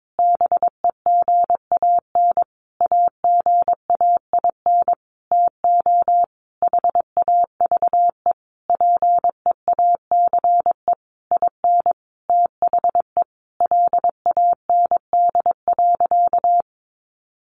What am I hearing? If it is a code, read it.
Morse code, heard as a keyed tone, about 22 wpm, 700 Hz. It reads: BEGAN AGAIN TO HAVE PEACE IN THE LAND.
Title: Sentences from Top 1500 Words 22wpm